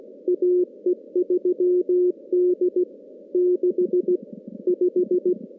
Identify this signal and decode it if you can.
Morse code, heard as a keyed tone, about 16 wpm, 410 Hz. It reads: AE3D 65
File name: contests / CW-SS-k5tr- -2100